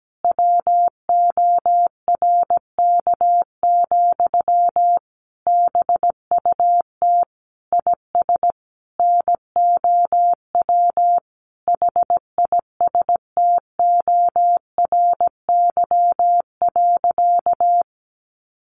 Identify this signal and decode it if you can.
Morse code, heard as a keyed tone, about 17 wpm, 700 Hz. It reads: WORK, BUT IS NOW HISTORY.
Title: Sentences from Top 1500 Words 17wpm